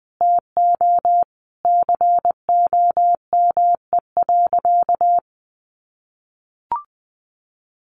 Morse code, heard as a keyed tone, about 19 wpm, 700 Hz. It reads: TO COME.
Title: A Sound Of Thunder 20wpm